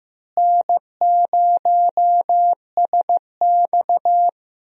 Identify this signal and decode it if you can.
Morse code, heard as a keyed tone, about 15 wpm, 700 Hz. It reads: N0SX
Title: USA Call Signs - 4 Characters - Repeated 3x - 15wpm